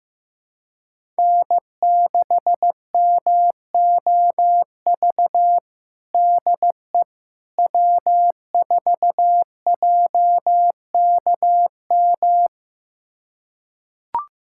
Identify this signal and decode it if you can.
Morse code, heard as a keyed tone, about 16 wpm, 700 Hz. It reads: N6MOV DE W4JKM
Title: USA Callsign DE Callsign - 5 Characters - 15wpm